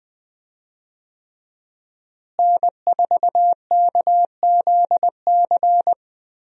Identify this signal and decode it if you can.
Morse code, heard as a keyed tone, about 20 wpm, 700 Hz. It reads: N4KZC
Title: USA Call Signs - 5 Characters - Repeated 3x - 20wpm